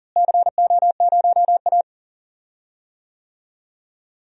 Morse code, heard as a keyed tone, about 40 wpm, 700 Hz. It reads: CO0A
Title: DXCC-100 Call Signs - 4 Characters - Repeated 2x - 40wpm